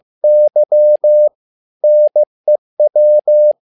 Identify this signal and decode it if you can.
Morse code, heard as a keyed tone, about 15 wpm, 600 Hz. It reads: Y NEW
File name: learning center / tower trouble / tower trouble 15wpm